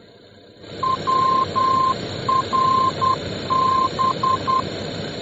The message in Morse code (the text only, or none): WRB